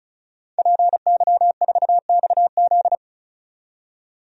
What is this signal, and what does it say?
Morse code, heard as a keyed tone, about 34 wpm, 700 Hz. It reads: PY4XZ
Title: DXCC-100 Call Signs - 5 Characters - 35wpm